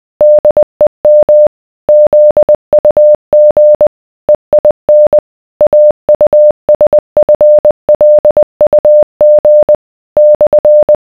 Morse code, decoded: DEM ZUG EIN AUSFLUG /